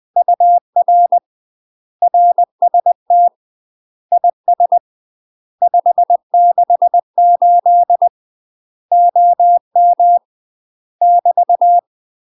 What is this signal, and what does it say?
Morse code, heard as a keyed tone, about 20 wpm, 700 Hz. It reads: UR RST IS 568 OM <BT>
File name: UK MASTER QSOs TRAINING / 20WPM / UK MASTER QSOs 0050